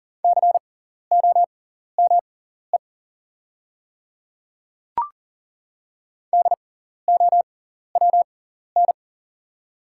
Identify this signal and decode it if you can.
Morse code, heard as a keyed tone, about 40 wpm, spaced as for 14 wpm, 700 Hz. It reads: COME DOWN
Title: 6x Character Spacing - Top 100 Words - 40wpm